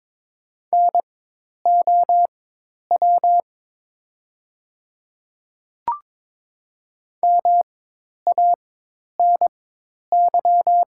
Morse code, heard as a keyed tone, about 22 wpm, spaced as for 10 wpm, 700 Hz. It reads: NOW MANY
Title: 4x Character Spacing - Top 500 Words - 22wpm